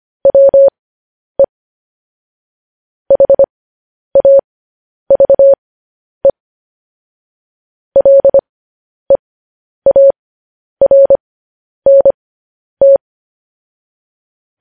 Morse code, heard as a keyed tone, about 25 wpm, spaced as for 10 wpm, 550 Hz. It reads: WE HAVE LEARNT
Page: news-headlines-in-morse-code-at-10-wpm